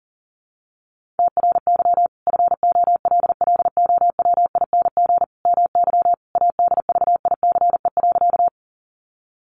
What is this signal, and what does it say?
Morse code, heard as a keyed tone, about 40 wpm, 700 Hz. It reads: TRY FOLLOWING MY ADVICE.
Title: Sentences from Top 1100 Words 40wpm